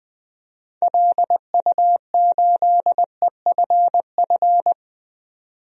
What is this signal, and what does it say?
Morse code, heard as a keyed tone, about 20 wpm, 700 Hz. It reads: LU8EFF
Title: DXCC-100 Call Signs - 6 Characters - 20wpm